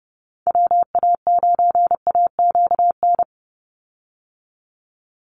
Morse code, heard as a keyed tone, about 30 wpm, 700 Hz. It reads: WA9AQN
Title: Call Signs - 30wpm